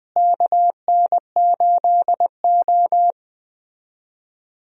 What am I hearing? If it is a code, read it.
Morse code, heard as a keyed tone, about 21 wpm, 700 Hz. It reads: KN8O
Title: USA Call Signs - 4 Characters - Repeated 2x - 20wpm